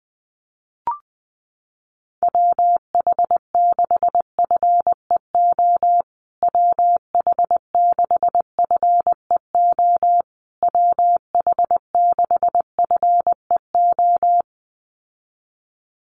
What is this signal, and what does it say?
Morse code, heard as a keyed tone, about 20 wpm, 700 Hz. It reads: WH6FEO WH6FEO WH6FEO
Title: USA Call Signs Repeated 3x 20wpm